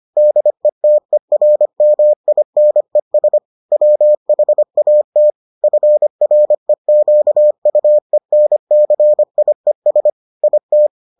Morse code, heard as a keyed tone, about 25 wpm, 600 Hz. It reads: DETERMINES WHAT FREQUENCIES IT